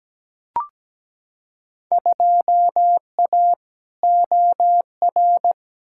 Morse code, heard as a keyed tone, about 17 wpm, 700 Hz. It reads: 2A OR
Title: Field Day Exchanges - 17wpm